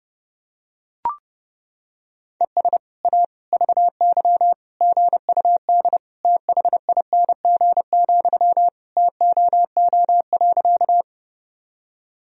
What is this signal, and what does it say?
Morse code, heard as a keyed tone, about 30 wpm, 700 Hz. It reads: ES A VY GUD THING, TOO.
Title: Sentences from Top 200 Words and Abbreviations 30wpm